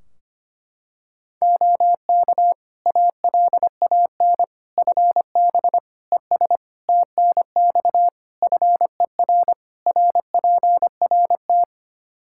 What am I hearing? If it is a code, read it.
Morse code, heard as a keyed tone, about 25 wpm, 700 Hz. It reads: OK ALAN FB ES TNX FER RPRT
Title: Practice QSO - Exchange 2 - Info - 25wpm